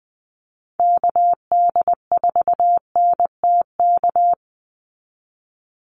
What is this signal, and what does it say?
Morse code, heard as a keyed tone, about 20 wpm, 700 Hz. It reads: KD4NTK
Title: USA Call Signs 20wpm